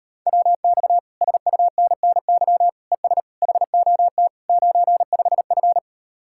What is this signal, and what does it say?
Morse code, heard as a keyed tone, about 39 wpm, 700 Hz. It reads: WX SUNNY ES HOT 95F